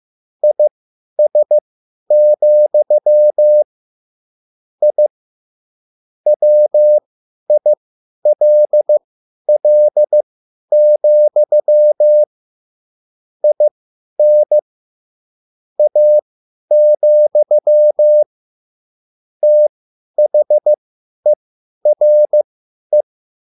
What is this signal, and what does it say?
Morse code, heard as a keyed tone, about 15 wpm, spaced as for 10 wpm, 600 Hz. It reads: IS, I WILL, IN A, THERE
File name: QSL / CW lesson